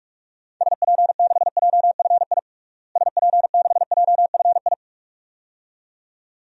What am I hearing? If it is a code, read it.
Morse code, heard as a keyed tone, about 46 wpm, 700 Hz. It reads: SP6JFI SP6JFI
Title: DXCC-100 Call Signs - 6 Characters - Repeated 2x - 45wpm